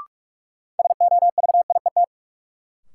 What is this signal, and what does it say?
Morse code, heard as a keyed tone, about 46 wpm, 700 Hz. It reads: SOVIET
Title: Sets of 5 Words Encore - New Word Drill - 45wpm